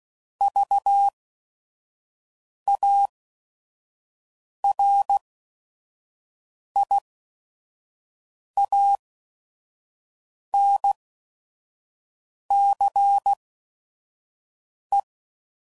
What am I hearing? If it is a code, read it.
Morse code, heard as a keyed tone, about 16 wpm, spaced as for 5 wpm, 800 Hz. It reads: VARIANCE
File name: Prefixes / By First Letter / Vs